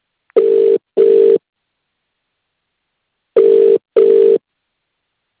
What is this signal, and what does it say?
Ringback tone (UK), 400 + 450 Hz.